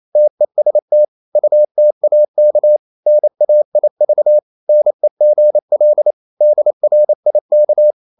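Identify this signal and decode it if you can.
Morse code, heard as a keyed tone, about 28 wpm, 600 Hz. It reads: TEST UTAK NAIV NEGL DRIK
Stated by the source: Danish short words sent in Morse at 28 wpm, from CW med Gnister.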